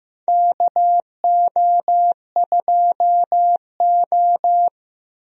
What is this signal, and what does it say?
Morse code, heard as a keyed tone, about 15 wpm, 700 Hz. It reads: KO2O
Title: USA Call Signs - 4 Characters - Repeated 2x - 15wpm